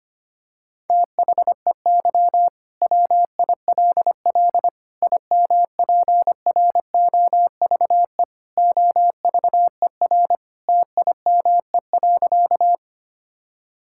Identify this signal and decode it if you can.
Morse code, heard as a keyed tone, about 25 wpm, 700 Hz. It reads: THEY WILL IMPROVE OVER TIME.
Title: Sentences from Top 1000 Words - Review Part 5 - 25wpm